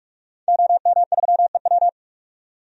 Morse code, heard as a keyed tone, about 46 wpm, 700 Hz. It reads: OM3EW
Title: DXCC-100 Call Signs - 5 Characters - 45wpm